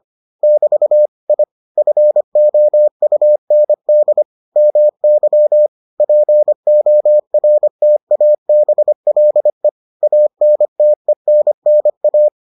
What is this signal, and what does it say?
Morse code, heard as a keyed tone, about 25 wpm, 600 Hz. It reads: <BT> I FOUND MY PORTABLE ANTENNA